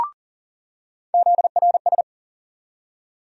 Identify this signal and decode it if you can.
Morse code, heard as a keyed tone, about 40 wpm, 700 Hz. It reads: ZRS
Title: USA Call Sign Postfixes - 3 Letters 40wpm